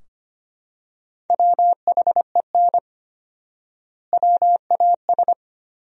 Morse code, heard as a keyed tone, about 25 wpm, 700 Hz. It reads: WHEN WAS
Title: 4x Word-Spacing - Sentences from Top 100 Words - 25wpm